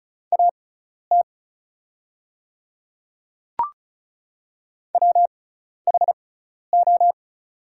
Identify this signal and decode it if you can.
Morse code, heard as a keyed tone, about 36 wpm, spaced as for 12 wpm, 700 Hz. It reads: AT WHO
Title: 6x Character Spacing - Top 100 Words - 35wpm